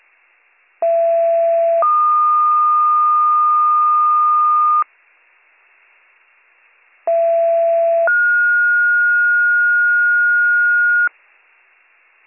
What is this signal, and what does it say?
Two-tone page, heard as a one-second tone, then a three-second tone, about 670 then 1220 Hz, about 670 then 1430 Hz. It is heard from 0.8 to 11.1 s.